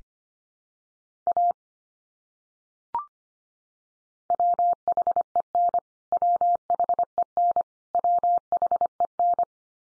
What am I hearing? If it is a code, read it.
Morse code, heard as a keyed tone, about 25 wpm, 700 Hz. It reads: A WHEN WHEN WHEN
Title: Top 100 Words 25wpm x3